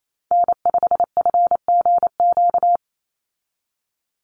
Morse code, heard as a keyed tone, about 28 wpm, 700 Hz. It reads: N5FGQ
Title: USA Call Signs 28wpm